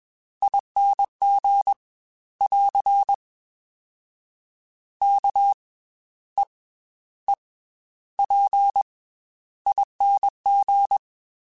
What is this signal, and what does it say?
Morse code, heard as a keyed tone, about 21 wpm, 800 Hz. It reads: ING <AR> K E E P ING
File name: Special INGs / Special INGs-mp3s / Slow ing @20